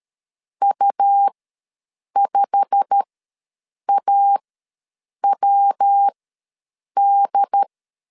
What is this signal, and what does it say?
Morse code, heard as a keyed tone, about 13 wpm, spaced as for 7 wpm, 800 Hz. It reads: U5AWD
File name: LEZIONI / 8 YJ45 / 8 YJ45 ws 12wpm 3146